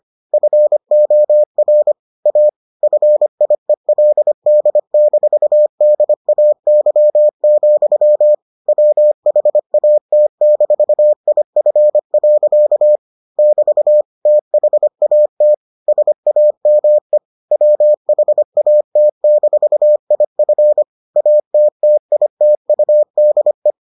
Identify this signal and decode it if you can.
Morse code, heard as a keyed tone, about 25 wpm, 600 Hz. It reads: FOR A FIELD-DAY, WHAT-IF. <BT> THAT SAME WHAT-IF ATTITUDE